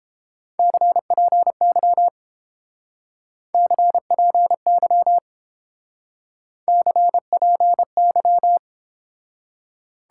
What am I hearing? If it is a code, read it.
Morse code, heard as a keyed tone, about 30 wpm, 700 Hz. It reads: CPY CPY CPY